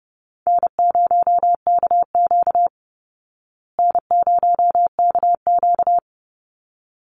Morse code, heard as a keyed tone, about 30 wpm, 700 Hz. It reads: N0KQ N0KQ
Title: Easy Call Signs Repeated 2x - 30wpm